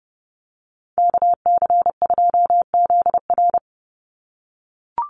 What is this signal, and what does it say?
Morse code, heard as a keyed tone, about 30 wpm, 700 Hz. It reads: KC2ZR
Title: USA Call Signs 30wpm